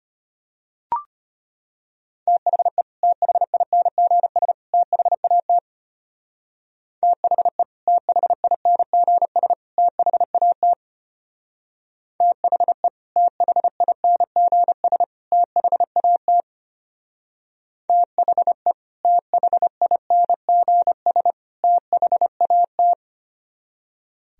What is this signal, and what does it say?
Morse code, heard as a keyed tone, about 31 wpm, 700 Hz. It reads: THE THINGS THAT THE THINGS THAT THE THINGS THAT THE THINGS THAT